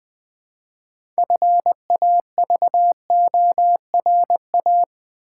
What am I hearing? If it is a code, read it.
Morse code, heard as a keyed tone, about 20 wpm, 700 Hz. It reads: FAVORA
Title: Build A Word - 3 Letters At A Time - 9-Letter Words - 20wpm